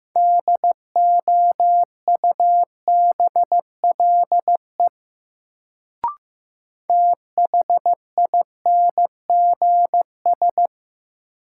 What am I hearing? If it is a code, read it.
Morse code, heard as a keyed tone, about 15 wpm, 700 Hz. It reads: DOUBLE THINGS
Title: Sentences from Top 1400 Words - New Word Drill Alt - 15wpm